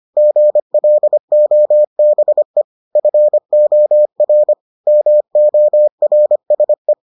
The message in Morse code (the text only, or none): GLOBE FOR MORSE